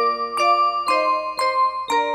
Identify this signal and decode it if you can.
There is no code.